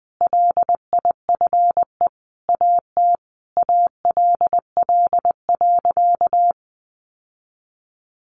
Morse code, heard as a keyed tone, about 20 wpm, 700 Hz. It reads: LIFE AT ALL.
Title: Sentences from Top 1100 Words 20wpm